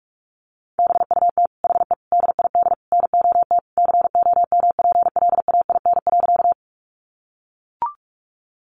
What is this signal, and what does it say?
Morse code, heard as a keyed tone, about 44 wpm, 700 Hz. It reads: BUT HE DID NOT COMPLAIN.
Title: Sentences from Top 1300 Words 45wpm